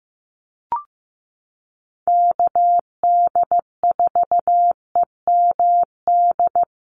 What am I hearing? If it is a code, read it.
Morse code, heard as a keyed tone, about 15 wpm, 700 Hz. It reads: KD4EMD
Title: USA Call Signs 15wpm